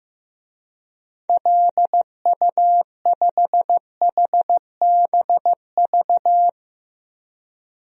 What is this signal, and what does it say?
Morse code, heard as a keyed tone, about 15 wpm, 700 Hz. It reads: LU5HBV
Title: DXCC-100 Call Signs - 6 Characters - Repeated 2x - 15wpm